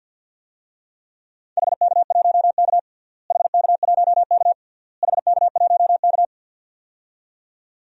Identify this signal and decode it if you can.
Morse code, heard as a keyed tone, about 53 wpm, 700 Hz. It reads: HK1K HK1K HK1K
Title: DXCC-100 Call Signs - 4 Characters - Repeated 3x - 50wpm